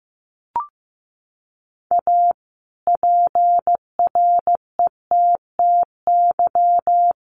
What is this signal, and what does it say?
Morse code, heard as a keyed tone, about 15 wpm, 700 Hz. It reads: A PRETTY
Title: Sentences from Top 1000 Words 15wpm